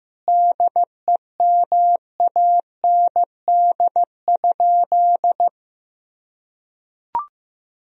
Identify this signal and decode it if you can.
Morse code, heard as a keyed tone, about 16 wpm, 700 Hz. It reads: DEMAND?
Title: Sentences from Top 1000 Words - Review Part 4 - 15wpm